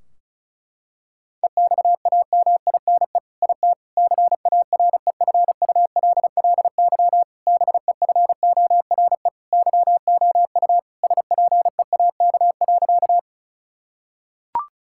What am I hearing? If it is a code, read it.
Morse code, heard as a keyed tone, about 36 wpm, 700 Hz. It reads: EXAMINE IT CAREFULLY BEFORE YOU SPEAK.
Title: Sentences from Top 1600 Words 35wpm